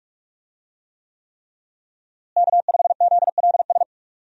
Morse code, heard as a keyed tone, about 46 wpm, 700 Hz. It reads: K5ZLS
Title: USA Call Signs - 5 Characters - Repeated 3x - 45wpm